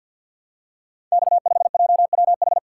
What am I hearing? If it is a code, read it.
Morse code, heard as a keyed tone, about 50 wpm, 700 Hz. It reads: X5JWH
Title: Code Groups - 5 Letter:Numbers - 50wpm